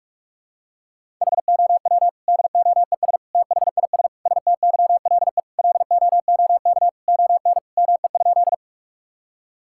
Morse code, heard as a keyed tone, about 41 wpm, 700 Hz. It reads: HOW DOES THIS STYLE LOOK ON ME?
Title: Sentences from Top 1000 Words - Review Part 5 - 45wpm